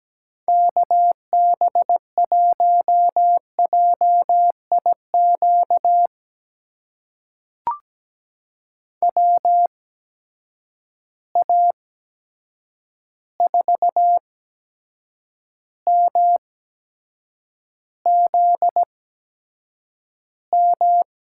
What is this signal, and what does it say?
Morse code, heard as a keyed tone, about 17 wpm, spaced as for 7 wpm, 700 Hz. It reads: KB1JIQ WA4MZM
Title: USA Call Signs - 6 Characters - 8x Character Spacing - 17wpm